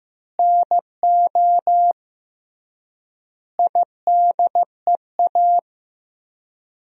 Morse code, heard as a keyed tone, about 16 wpm, 700 Hz. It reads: NO IDEA
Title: 3x Word-Spacing - Sentences from Top 300 Words - 15wpm